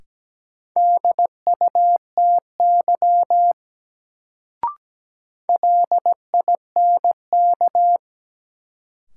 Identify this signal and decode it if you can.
Morse code, heard as a keyed tone, about 17 wpm, 700 Hz. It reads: DUTY LINK
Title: Sentences from Top 1500 Words - New Word Drill Alt - 17wpm